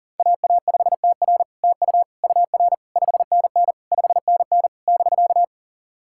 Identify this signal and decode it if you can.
Morse code, heard as a keyed tone, about 42 wpm, 700 Hz. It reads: AA5TR TU UR 5NN 5NN <BK>
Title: SOTA QSO Practice 40wpm